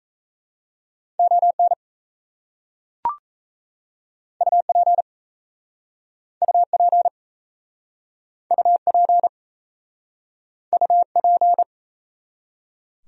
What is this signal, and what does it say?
Morse code, heard as a keyed tone, about 39 wpm, 700 Hz. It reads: ON UP UP UP UP